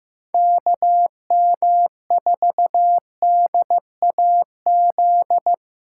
Morse code, heard as a keyed tone, about 15 wpm, 700 Hz. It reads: KM4DAZ
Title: USA Callsign DE Callsign 15wpm